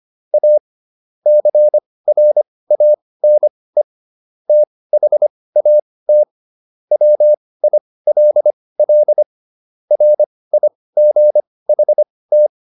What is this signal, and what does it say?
Morse code, heard as a keyed tone, about 25 wpm, spaced as for 18 wpm, 600 Hz. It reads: A CRANE THAT WILL RIGHT